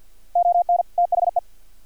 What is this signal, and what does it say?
Morse code, heard as a keyed tone, about 50 wpm, 700 Hz. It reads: ON THE
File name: archived / CRSnet